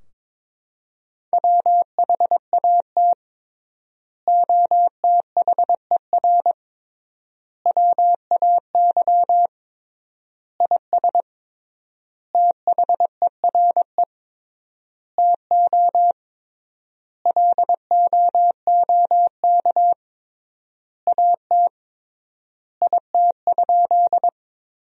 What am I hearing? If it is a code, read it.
Morse code, heard as a keyed tone, about 22 wpm, 700 Hz. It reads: WHAT OTHER WAY IS THERE TO LOOK AT IT?
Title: 3x Word-Spacing - Sentences from Top 100 Words - 22wpm